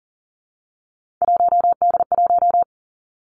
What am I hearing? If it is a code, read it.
Morse code, heard as a keyed tone, about 40 wpm, 700 Hz. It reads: 1D1
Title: Three Challenging Letter-Number Combinations 40wpm